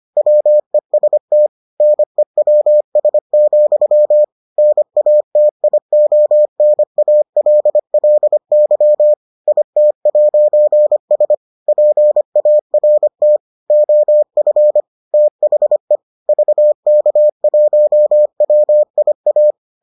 Morse code, heard as a keyed tone, about 25 wpm, 600 Hz. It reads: WEST NEWS, NATIONALLY IT'S PART OF THE VK1WIA